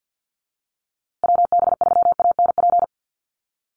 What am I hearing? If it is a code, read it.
Morse code, heard as a keyed tone, about 48 wpm, 700 Hz. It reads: WB3ANP
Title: USA Call Signs Repeated 2x 50wpm